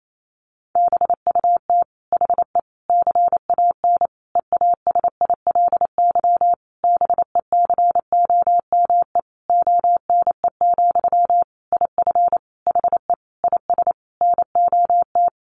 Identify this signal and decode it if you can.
Morse code, heard as a keyed tone, about 28 wpm, 700 Hz. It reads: BUT HE CAN EASILY BECOME ONE, IF HE IS NOT
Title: Sentences from Top 900 Words 28wpm